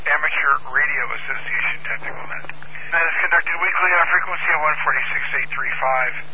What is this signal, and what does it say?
Emergency Alert System (SAME) tones (header bursts), heard as harsh buzzing bursts, with speech. It is heard from 0.7 to 5.7 s.